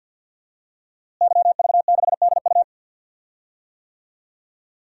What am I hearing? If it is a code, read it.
Morse code, heard as a keyed tone, about 50 wpm, 700 Hz. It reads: YV6DU
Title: DXCC-100 Call Signs - 5 Characters - Repeated 3x - 50wpm